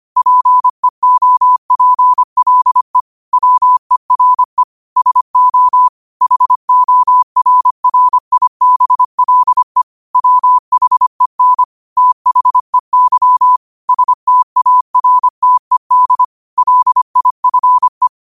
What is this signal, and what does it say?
Morse code, heard as a keyed tone, about 25 wpm, 1000 Hz. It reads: PEOPLE WERE SO HORRIBLE WHEN THEY STARTED LIFE